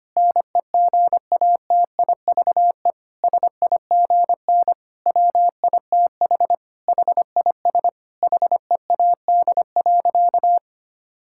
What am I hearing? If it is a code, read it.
Morse code, heard as a keyed tone, about 25 wpm, 700 Hz. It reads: NEGATIVE SIGN WITH HIS HEAD.
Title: Sentences from Top 1000 Words - Review Part 6 - 25wpm